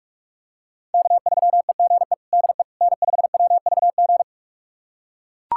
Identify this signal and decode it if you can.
Morse code, heard as a keyed tone, about 41 wpm, 700 Hz. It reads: K3EGE DE N5WVG
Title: USA Callsign DE Callsign - 5 Characters - 45wpm